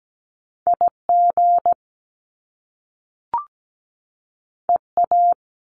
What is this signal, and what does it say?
Morse code, heard as a keyed tone, about 17 wpm, 700 Hz. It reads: IG EA
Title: Top Two Letter Combinations in Words 17wpm